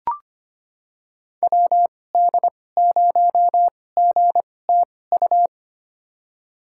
Morse code, heard as a keyed tone, about 25 wpm, spaced as for 18 wpm, 700 Hz. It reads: WD0GTU
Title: USA Call Signs - 6 Characters - 2x Character Spacing - 25wpm